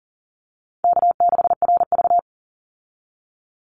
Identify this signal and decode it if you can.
Morse code, heard as a keyed tone, about 40 wpm, 700 Hz. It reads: K6RV
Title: Easy Call Signs - 40wpm